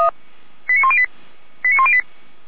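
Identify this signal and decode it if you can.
Five-tone selective call (ZVEI), heard as quick runs of five tones, tones of about 70 ms. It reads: 89108 89108
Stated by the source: ZVEI five-tone call received on an SDR.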